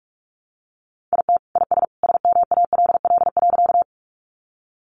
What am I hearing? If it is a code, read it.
Morse code, heard as a keyed tone, about 44 wpm, 700 Hz. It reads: IT IS SMALL.
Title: Additional Easy Phrases 45wpm